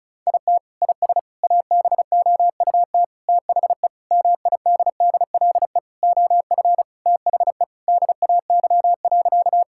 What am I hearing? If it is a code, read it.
Morse code, heard as a keyed tone, about 36 wpm, 700 Hz. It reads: IT IS ABOUT THE MIDDLE OF THE DAY.
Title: Sentences from Top 1000 Words - Review Part 6 - 35wpm